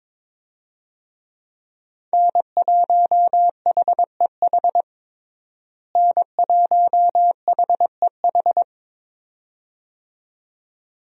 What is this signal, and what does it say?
Morse code, heard as a keyed tone, about 22 wpm, 700 Hz. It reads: N1HEH N1HEH
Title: USA Call Signs - 5 Characters - Repeated 2x - 22wpm